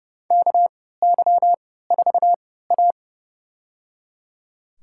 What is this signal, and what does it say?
Morse code, heard as a keyed tone, about 30 wpm, spaced as for 17 wpm, 700 Hz. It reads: KY4A
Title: USA Call Signs - 4 Characters - 3x Character Spacing - 30wpm